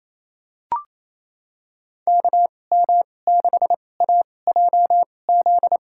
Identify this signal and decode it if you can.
Morse code, heard as a keyed tone, about 28 wpm, spaced as for 20 wpm, 700 Hz. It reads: KM6AJZ
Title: USA Call Signs - 6 Characters - 2x Character Spacing - 28wpm